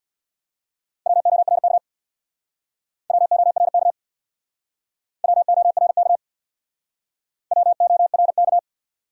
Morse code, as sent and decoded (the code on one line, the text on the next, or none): .-- --- .-. -.- / .-- --- .-. -.- / .-- --- .-. -.- / .-- --- .-. -.-
WORK WORK WORK WORK